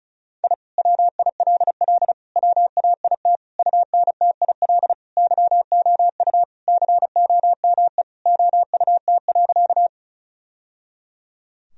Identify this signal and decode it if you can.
Morse code, heard as a keyed tone, about 36 wpm, 700 Hz. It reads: I WILL WAIT UNTIL YOU COME OUT.
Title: Sentences from Top 1000 Words - Review Part 1 - 35wpm